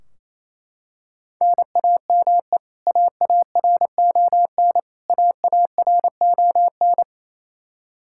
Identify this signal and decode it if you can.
Morse code, heard as a keyed tone, about 27 wpm, 700 Hz. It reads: NAME AARON AARON
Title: Practice QSO - Exchange 1 - Beginning - 28wpm